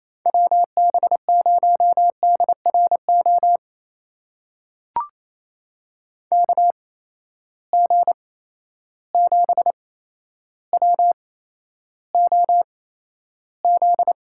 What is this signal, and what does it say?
Morse code, heard as a keyed tone, about 28 wpm, spaced as for 12 wpm, 700 Hz. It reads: WB0DRO KG7WOZ
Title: USA Call Signs - 6 Characters - 8x Character Spacing - 28wpm